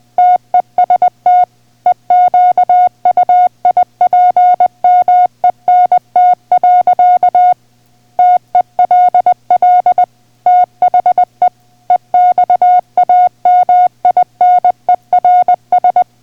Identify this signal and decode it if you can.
Morse code, heard as a keyed tone, about 20 wpm, 720 Hz. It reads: TEST EQUIPMENT. TELL THE EXAMINERS